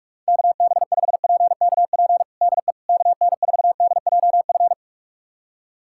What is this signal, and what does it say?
Morse code, heard as a keyed tone, about 46 wpm, 700 Hz. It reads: KB5PKP DE KN4DJF
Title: USA Callsign DE Callsign - 6 Characters - 45wpm